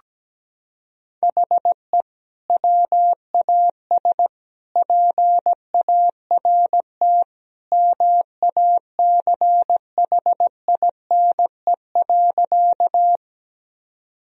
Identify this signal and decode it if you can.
Morse code, heard as a keyed tone, about 17 wpm, 700 Hz. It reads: HE WAS PART MACHINE.
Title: Sentences from Top 1000 Words - Review Part 2 - 17wpm